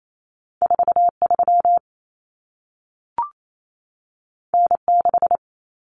Morse code, heard as a keyed tone, about 28 wpm, 700 Hz. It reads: 43 N6
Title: Common Cut Numbers - 2-Digit 28wpm